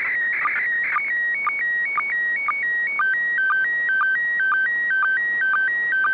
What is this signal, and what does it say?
SSTV (slow-scan television), PD 120.
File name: ik1yrc / AUDIO FOR QRZ / FLUSSO ISS 29 01 2021 05 00 00 UTC